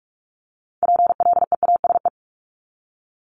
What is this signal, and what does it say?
Morse code, heard as a keyed tone, about 46 wpm, 700 Hz. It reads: PLEASE
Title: Multi-letter Phonemes 45wpm